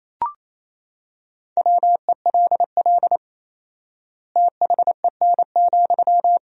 Morse code, heard as a keyed tone, about 28 wpm, 700 Hz. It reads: WELL THEN,
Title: 4x Word-Spacing - Sentences from Top 500 Words - 28wpm